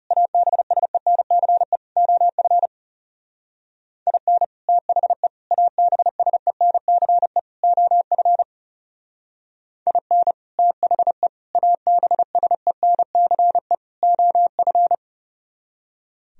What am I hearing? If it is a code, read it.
Morse code, heard as a keyed tone, about 37 wpm, 700 Hz. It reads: ABSENCE OF IN THE ABSENCE OF IN THE ABSENCE OF
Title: Speed-Racing - Sets of 4 Words 30wpm